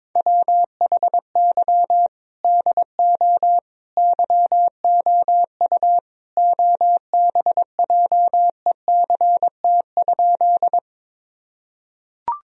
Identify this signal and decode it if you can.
Morse code, heard as a keyed tone, about 22 wpm, 700 Hz. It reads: WHY DO YOU OBJECT?
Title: Sentences from Top 1000 Words - Review Part 3 - 22wpm